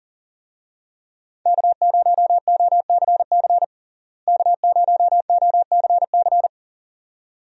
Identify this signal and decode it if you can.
Morse code, heard as a keyed tone, about 40 wpm, 700 Hz. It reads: K0OCC K0OCC
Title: USA Call Signs - 5 Characters - Repeated 2x - 40wpm